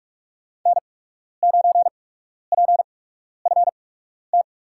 Morse code, heard as a keyed tone, about 44 wpm, spaced as for 12 wpm, 700 Hz. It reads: N9PFT
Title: USA Call Signs - 8x Character Spacing - 45wpm